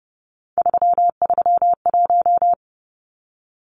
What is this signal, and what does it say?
Morse code, heard as a keyed tone, about 30 wpm, 700 Hz. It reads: 331
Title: Numbers - 3-Digit 30wpm